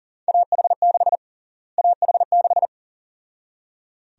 Morse code, heard as a keyed tone, about 40 wpm, 700 Hz. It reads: AH6 AH6
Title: USA Call Sign Prefixes - Repeated 2x - 40wpm